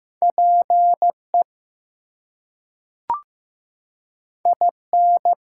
Morse code, heard as a keyed tone, about 15 wpm, 700 Hz. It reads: PE IN
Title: US States and Canadian Provinces 15wpm